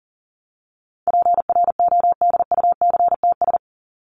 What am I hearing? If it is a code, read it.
Morse code, heard as a keyed tone, about 40 wpm, 700 Hz. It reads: PRODUCTS
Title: ES:S Suffix - Top 100 Words with Base Words 40wpm